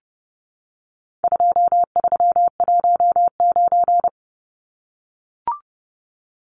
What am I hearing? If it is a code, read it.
Morse code, heard as a keyed tone, about 30 wpm, 700 Hz. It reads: 2319
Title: Numbers - 4-Digit 30wpm